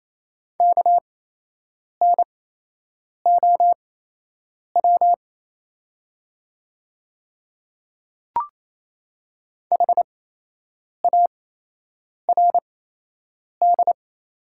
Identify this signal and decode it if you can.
Morse code, heard as a keyed tone, about 27 wpm, spaced as for 8 wpm, 700 Hz. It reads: KNOW HARD
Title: 8x Character Spacing - Top 300 Words - 28wpm